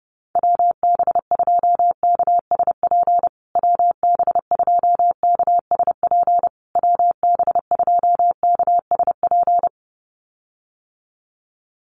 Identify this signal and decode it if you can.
Morse code, heard as a keyed tone, about 30 wpm, 700 Hz. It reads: WB2KSP WB2KSP WB2KSP
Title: USA Call Signs Repeated 3x 30wpm